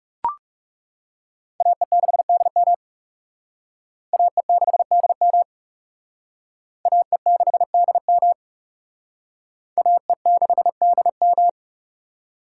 Morse code, heard as a keyed tone, about 36 wpm, 700 Hz. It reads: AE6DM AE6DM AE6DM AE6DM
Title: Speed Racing - USA 2x2 Call Signs 30wpm